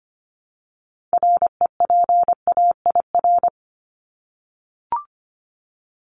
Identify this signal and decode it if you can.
Morse code, heard as a keyed tone, about 25 wpm, 700 Hz. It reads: REPAIR
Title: Top 1501-1750 Words 25wpm